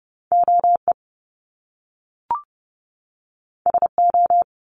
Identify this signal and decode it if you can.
Morse code, heard as a keyed tone, about 30 wpm, 700 Hz. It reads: OE SO
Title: Top Two Letter Combinations in Words 30wpm